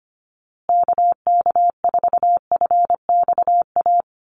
Morse code, heard as a keyed tone, about 25 wpm, 700 Hz. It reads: KK4FXA
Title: USA Call Signs Repeated 2x 25wpm